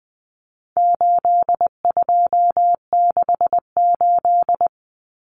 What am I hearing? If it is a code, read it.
Morse code, heard as a keyed tone, about 20 wpm, 700 Hz. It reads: 8268
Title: Numbers - 4-Digit 20wpm